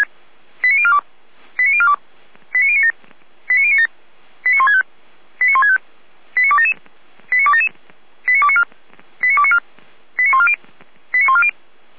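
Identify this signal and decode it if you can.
Five-tone selective call (ZVEI), heard as quick runs of five tones, tones of about 80 ms. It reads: 89042 89042 89097 89097 89165 89165 89277 89277 89294 89294 89140 89140
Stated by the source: ZVEI five-tone selective call 89173 received by SDR, recorded by the Feuerwehr Orlamünde fire brigade.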